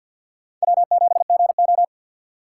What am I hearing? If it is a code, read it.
Morse code, heard as a keyed tone, about 53 wpm, 700 Hz. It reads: W7GO
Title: USA Call Signs - 6x Character Spacing - 50wpm